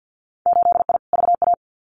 Morse code, heard as a keyed tone, about 50 wpm, 700 Hz. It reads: 8I VA
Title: Winter Field Day Exchanges Encore 50wpm